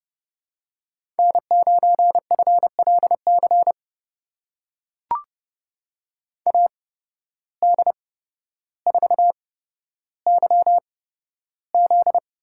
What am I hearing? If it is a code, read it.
Morse code, heard as a keyed tone, about 29 wpm, spaced as for 13 wpm, 700 Hz. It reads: N9FLC AD4YZ
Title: USA Call Signs - 5 Characters - 8x Character Spacing - 30wpm